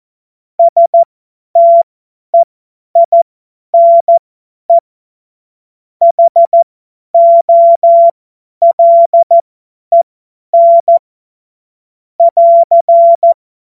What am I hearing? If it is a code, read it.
Morse code, heard as a keyed tone, about 14 wpm, spaced as for 10 wpm, 680 Hz. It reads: STEINE HOLEN <AR>
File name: Morsekurs / Witze / Witz 21. Februar 2023 50 F70